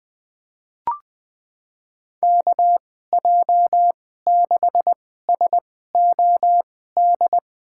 Morse code, heard as a keyed tone, about 20 wpm, spaced as for 14 wpm, 700 Hz. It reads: KJ6SOD
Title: USA Call Signs - 6 Characters - 2x Character Spacing - 20wpm